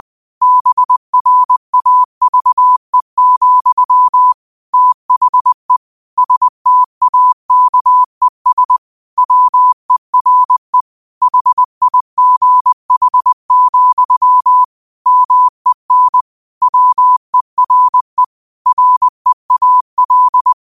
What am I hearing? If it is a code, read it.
Morse code, heard as a keyed tone, about 20 wpm, 1000 Hz. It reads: BRAVE, THE STAKES WERE HIGH, MEN WERE REAL
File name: qotdfiles / 20wpm